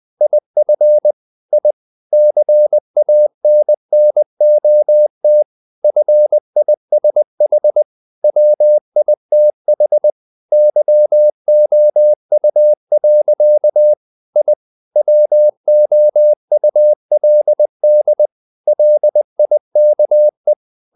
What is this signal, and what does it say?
Morse code, heard as a keyed tone, about 20 wpm, 600 Hz. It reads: IF I CANNOT FISH WITH YOU. I WOULD LIKE
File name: the old man and the sea / the old man and the sea part 1 20wpm